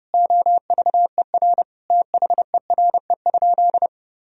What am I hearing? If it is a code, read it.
Morse code, heard as a keyed tone, about 32 wpm, 700 Hz. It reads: OVER THERE?
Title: Sentences from Top 1000 Words - Review Part 2 - 30wpm